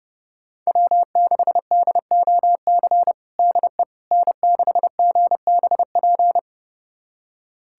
Morse code, heard as a keyed tone, about 30 wpm, 700 Hz. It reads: W6DOC DE N6GBP
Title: USA Callsign DE Callsign - 5 Characters - 30wpm